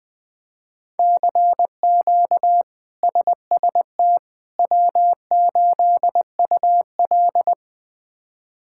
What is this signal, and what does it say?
Morse code, heard as a keyed tone, about 20 wpm, 700 Hz. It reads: CQ SST W8UL
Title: K1USN SST QSO Practice - 20wpm